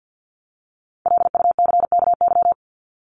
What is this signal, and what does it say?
Morse code, heard as a keyed tone, about 50 wpm, 700 Hz. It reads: LUCKY
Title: Top 1751-2000 Words 50wpm